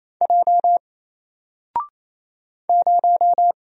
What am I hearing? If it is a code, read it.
Morse code, heard as a keyed tone, about 27 wpm, 700 Hz. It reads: J 0
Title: Single Character (Letter:Number:Punctuation:Prosign) 28wpm